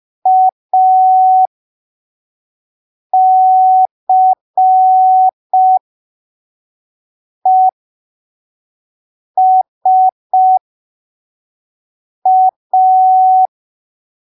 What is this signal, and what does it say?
Morse code, heard as a keyed tone, about 5 wpm, spaced as for 3 wpm, 750 Hz. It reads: ACESA
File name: arquivos / texto / texto 08 05ppm